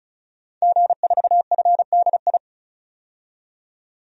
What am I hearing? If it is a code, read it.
Morse code, heard as a keyed tone, about 36 wpm, 700 Hz. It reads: G4FDI
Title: DXCC-100 Call Signs - 5 Characters - 35wpm